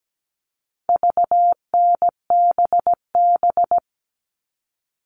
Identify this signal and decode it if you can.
Morse code, heard as a keyed tone, about 17 wpm, 700 Hz. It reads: VNBB
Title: All Cut Numbers - 4-Digit 17wpm